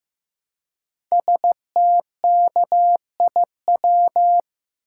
Morse code, heard as a keyed tone, about 15 wpm, 700 Hz. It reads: STKIW
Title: Code Groups - 5 Letters - 15wpm